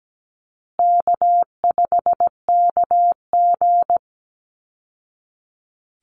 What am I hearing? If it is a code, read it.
Morse code, heard as a keyed tone, about 17 wpm, 700 Hz. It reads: K5KG
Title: Easy Call Signs Repeated 3x - 17wpm